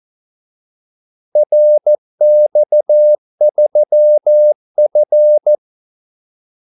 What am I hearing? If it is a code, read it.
Morse code, heard as a keyed tone, about 14 wpm, 600 Hz. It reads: RX3F
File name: Callsigns-14wpm